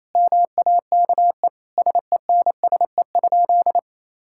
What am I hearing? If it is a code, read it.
Morse code, heard as a keyed tone, about 28 wpm, 700 Hz. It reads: MAKE SENSE?
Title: Sentences from Top 1000 Words - Review Part 5 - 28wpm